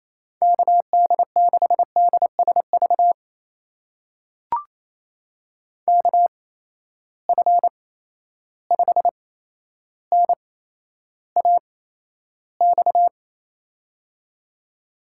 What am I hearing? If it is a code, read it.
Morse code, heard as a keyed tone, about 28 wpm, spaced as for 12 wpm, 700 Hz. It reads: KD6DSV KF5NAX
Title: USA Call Signs - 6 Characters - 8x Character Spacing - 28wpm